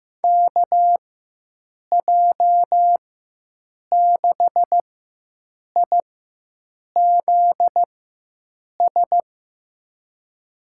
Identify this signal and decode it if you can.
Morse code, heard as a keyed tone, about 15 wpm, spaced as for 7 wpm, 700 Hz. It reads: KJ6IZS
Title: USA Call Signs - 4x Character Spacing - 15wpm